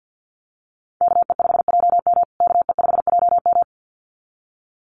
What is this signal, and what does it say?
Morse code, heard as a keyed tone, about 50 wpm, 700 Hz. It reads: KE5JM KE5JM
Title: USA Call Signs Repeated 2x 50wpm